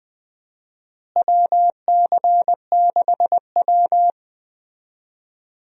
Morse code, heard as a keyed tone, about 21 wpm, 700 Hz. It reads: WC6W
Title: USA Call Signs - 4 Characters - Repeated 2x - 20wpm